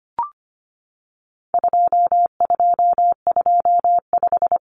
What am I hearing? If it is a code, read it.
Morse code, heard as a keyed tone, about 25 wpm, 700 Hz. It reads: 2225
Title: Common Cut Numbers - 4-Digit 25wpm